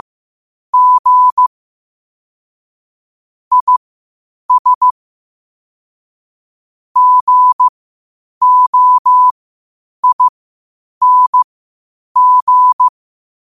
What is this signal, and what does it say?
Morse code, heard as a keyed tone, about 15 wpm, spaced as for 8 wpm, 1000 Hz. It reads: G IS GOING
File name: qotdfiles / 8wpm